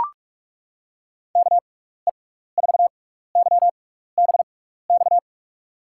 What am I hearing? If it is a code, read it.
Morse code, heard as a keyed tone, about 43 wpm, spaced as for 15 wpm, 700 Hz. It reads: KE4YBX
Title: USA Call Signs - 6 Characters - 6x Character Spacing - 45wpm